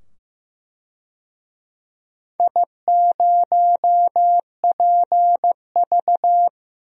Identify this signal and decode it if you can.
Morse code, heard as a keyed tone, about 15 wpm, 700 Hz. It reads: I0PV
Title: DXCC-100 Call Signs - 4 Characters - 15wpm